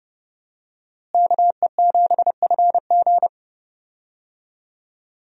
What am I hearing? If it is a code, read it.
Morse code, heard as a keyed tone, about 30 wpm, 700 Hz. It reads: KE7FG
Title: USA Call Signs - 5 Characters - Repeated 2x - 30wpm